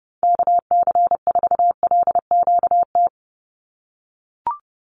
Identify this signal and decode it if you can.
Morse code, heard as a keyed tone, about 30 wpm, 700 Hz. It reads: KC4LQT
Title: USA Call Signs 30wpm